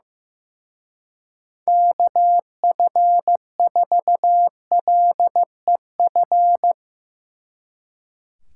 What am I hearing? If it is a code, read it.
Morse code, heard as a keyed tone, about 16 wpm, 700 Hz. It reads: KF4LEF
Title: Speed Racing - USA 2x3 Call Signs 15wpm